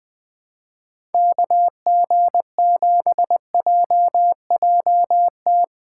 Morse code, heard as a keyed tone, about 20 wpm, 700 Hz. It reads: KG7JJT
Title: USA Callsign DE Callsign - 6 Characters - 20wpm